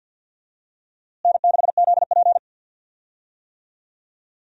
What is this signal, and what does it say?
Morse code, heard as a keyed tone, about 52 wpm, 700 Hz. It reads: N6ZP
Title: USA Call Signs - 4 Characters - Repeated 3x - 50wpm